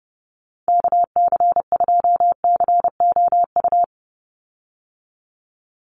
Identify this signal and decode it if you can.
Morse code, heard as a keyed tone, about 28 wpm, 700 Hz. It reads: KC2COU
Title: USA Call Signs 30wpm